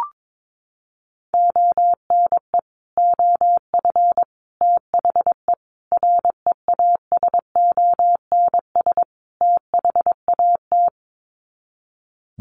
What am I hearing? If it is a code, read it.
Morse code, heard as a keyed tone, about 22 wpm, 700 Hz. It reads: ONE OF THE REASONS THAT
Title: Sets of 5 Words Encore 22wpm